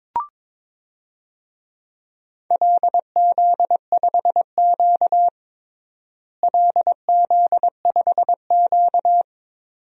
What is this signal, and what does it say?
Morse code, heard as a keyed tone, about 21 wpm, 700 Hz. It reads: LZ5Q LZ5Q
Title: DXCC-100 Call Signs - 4 Characters - Repeated 2x - 22wpm